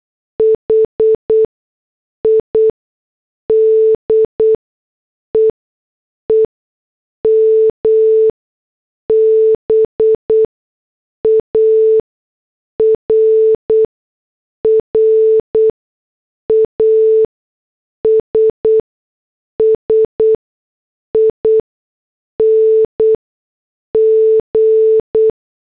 Morse code, heard as a keyed tone, about 8 wpm, spaced as for 6 wpm, 440 Hz. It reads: HIDE EMBARRASSING
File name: morse / headlines